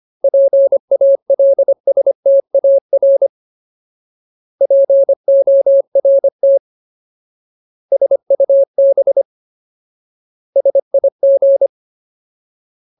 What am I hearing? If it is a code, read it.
Morse code, heard as a keyed tone, about 25 wpm, 550 Hz. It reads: PALSTAR PORT SUB SIG
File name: Ultimate Word List / 25WPM Ultimate